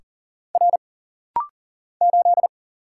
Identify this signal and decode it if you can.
Morse code, heard as a keyed tone, about 40 wpm, 700 Hz. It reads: R 8
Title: Single Letter-Number - Rapid-Fire 40wpm